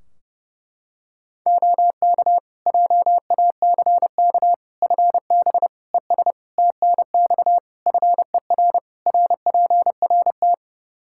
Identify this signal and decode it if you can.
Morse code, heard as a keyed tone, about 30 wpm, 700 Hz. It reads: OK JACK FB ES TNX FER RPRT
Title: Practice QSO - Exchange 2 - Info - 30wpm